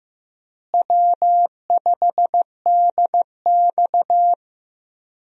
Morse code, heard as a keyed tone, about 15 wpm, 700 Hz. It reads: W5DX
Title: USA Call Signs - 4 Characters - Repeated 2x - 15wpm